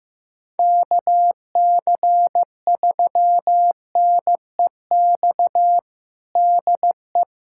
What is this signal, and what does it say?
Morse code, heard as a keyed tone, about 15 wpm, 700 Hz. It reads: KC3NEX DE
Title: USA Callsign DE Callsign - 6 Characters - 15wpm